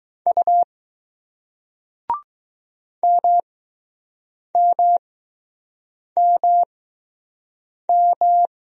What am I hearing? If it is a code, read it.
Morse code, heard as a keyed tone, about 20 wpm, spaced as for 6 wpm, 700 Hz. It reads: U MMMM